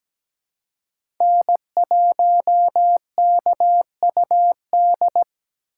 Morse code, heard as a keyed tone, about 17 wpm, 700 Hz. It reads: N1KUD
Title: USA Call Signs - 5 Characters - 4x Character Spacing - 17wpm